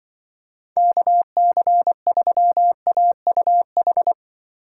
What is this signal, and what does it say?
Morse code, heard as a keyed tone, about 23 wpm, 700 Hz. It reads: KC3AUH
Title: Speed Racing - USA 2x3 Call Signs 20wpm